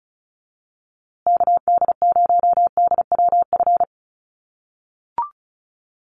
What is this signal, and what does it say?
Morse code, heard as a keyed tone, about 34 wpm, 700 Hz. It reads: KD0DWF
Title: USA Call Signs Repeated 2x 35wpm